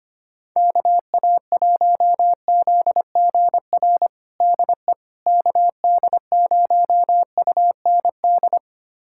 Morse code, heard as a keyed tone, about 25 wpm, 700 Hz. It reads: KA1ZGR DE KD0UND
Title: USA Callsign DE Callsign - 6 Characters - 25wpm